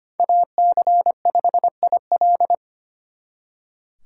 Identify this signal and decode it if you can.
Morse code, heard as a keyed tone, about 26 wpm, 700 Hz. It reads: AC5IL
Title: Speed Racing - USA 2x2 Call Signs 25wpm